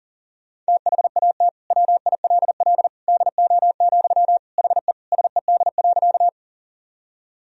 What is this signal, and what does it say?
Morse code, heard as a keyed tone, about 37 wpm, 700 Hz. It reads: THAT WILL DO, HE SED.
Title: Sentences from Top 100 Words and Abbreviations 40wpm